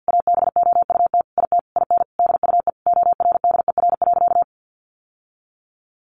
Morse code, heard as a keyed tone, about 49 wpm, 700 Hz. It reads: ABOUT IT IN DUE ORDER.
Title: Sentences from Top 900 Words 50wpm